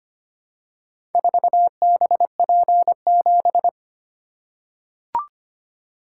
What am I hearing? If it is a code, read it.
Morse code, heard as a keyed tone, about 25 wpm, 700 Hz. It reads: 4BP7
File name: Four Letter-Number Combinations - 25wpm